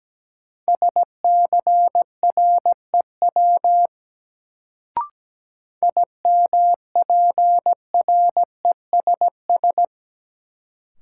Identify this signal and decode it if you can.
Morse code, heard as a keyed tone, about 17 wpm, 700 Hz. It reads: SCREW IMPRESS
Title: Sentences from Top 1600 Words - New Word Drill Alt - 17wpm